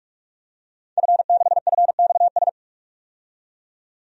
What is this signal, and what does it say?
Morse code, heard as a keyed tone, about 47 wpm, 700 Hz. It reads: F6FXS
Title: DXCC-100 Call Signs - 5 Characters - 45wpm